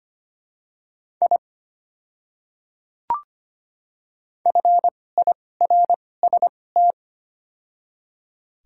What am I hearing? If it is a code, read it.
Morse code, heard as a keyed tone, about 25 wpm, spaced as for 18 wpm, 700 Hz. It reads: I FIRST
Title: 2x Character Spacing - Top 100 Words - 25wpm